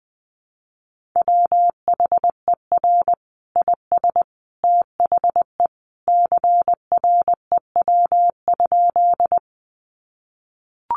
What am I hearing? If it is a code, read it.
Morse code, heard as a keyed tone, about 20 wpm, 700 Hz. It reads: WHER IS THE CREW?
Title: Sentences from Top 1500 Words 20wpm